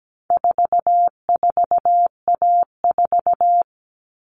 Morse code, heard as a keyed tone, about 17 wpm, 700 Hz. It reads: 44A4
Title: All Cut Numbers - 4-Digit 17wpm